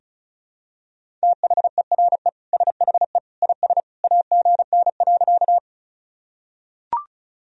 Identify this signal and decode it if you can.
Morse code, heard as a keyed tone, about 36 wpm, 690 Hz. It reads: THERE SHE IS AGN.
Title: Sentences from Top 200 Words and Abbreviations 35wpm